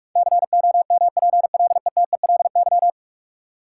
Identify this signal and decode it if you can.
Morse code, heard as a keyed tone, about 46 wpm, 700 Hz. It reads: COMPLETELY
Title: Sentences from Top 1100 Words - New Word Drill Alt - 45wpm